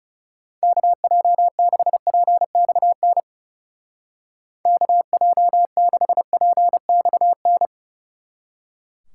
Morse code, heard as a keyed tone, about 34 wpm, 700 Hz. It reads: KJ6PXN KJ6PXN
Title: Speed Racing - USA 2x3 Call Signs 30wpm